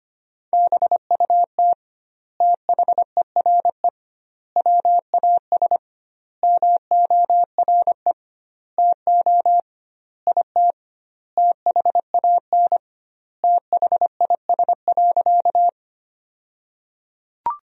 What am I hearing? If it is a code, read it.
Morse code, heard as a keyed tone, about 25 wpm, 700 Hz. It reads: BUT THERE WAS MORE TO IT THAN THIS.
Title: 2x Word-Spacing - Sentences from Top 100 Words - 25wpm